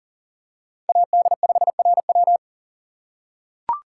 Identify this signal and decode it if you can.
Morse code, heard as a keyed tone, about 39 wpm, 700 Hz. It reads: AD5RW
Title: USA Call Signs - 5 Characters - 4x Character Spacing - 40wpm